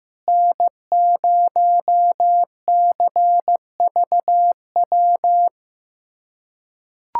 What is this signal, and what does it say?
Morse code, heard as a keyed tone, about 15 wpm, 700 Hz. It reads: N0CVW
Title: USA Callsign DE Callsign - 5 Characters - 15wpm